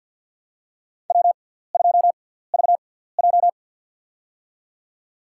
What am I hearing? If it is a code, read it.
Morse code, heard as a keyed tone, about 47 wpm, spaced as for 17 wpm, 700 Hz. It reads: W2VJ
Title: USA Call Signs - 4 Characters - 6x Character Spacing - 50wpm